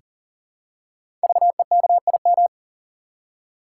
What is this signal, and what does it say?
Morse code, heard as a keyed tone, about 39 wpm, 700 Hz. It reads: VEKIM
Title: Code Groups - 5 Letters - 40wpm